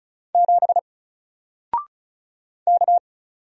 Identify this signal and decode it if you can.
Morse code, heard as a keyed tone, about 36 wpm, 700 Hz. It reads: 7 K
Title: Single Letter-Number Alt 35wpm